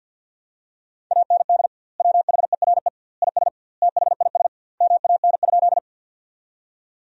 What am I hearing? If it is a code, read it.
Morse code, heard as a keyed tone, about 51 wpm, 700 Hz. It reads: AND WHERE IS THIS MAN?
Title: 2x Word-Spacing - Sentences from Top 200 Words - 50wpm